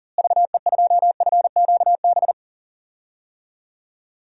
Morse code, heard as a keyed tone, about 40 wpm, 700 Hz. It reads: VE2FQB
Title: DXCC-100 Call Signs - 6 Characters - Repeated 2x - 40wpm